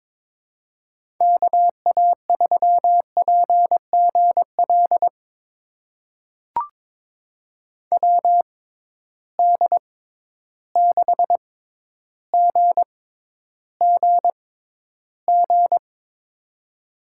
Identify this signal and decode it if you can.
Morse code, heard as a keyed tone, about 21 wpm, spaced as for 11 wpm, 700 Hz. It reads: KA3PGL WD6GGG
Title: USA Call Signs - 6 Characters - 6x Character Spacing - 22wpm